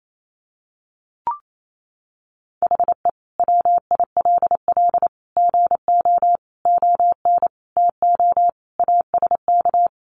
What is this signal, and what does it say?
Morse code, heard as a keyed tone, about 28 wpm, 700 Hz. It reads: HE WILL GO ON TO ASK
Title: Sentences from Top 1500 Words 28wpm